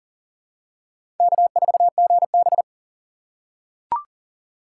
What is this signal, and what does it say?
Morse code, heard as a keyed tone, about 40 wpm, 700 Hz. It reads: K4GB
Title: USA Call Signs - 4x Character Spacing - 40wpm